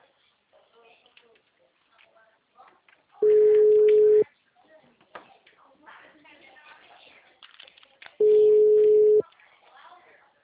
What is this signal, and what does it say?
Ringback tone (Europe, 425 Hz), with speech.